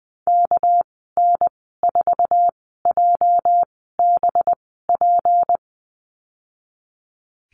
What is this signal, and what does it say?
Morse code, heard as a keyed tone, about 20 wpm, spaced as for 14 wpm, 700 Hz. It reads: KN4JBP
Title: 207 Lesson 33g - Letter j - Call Signs 20wpm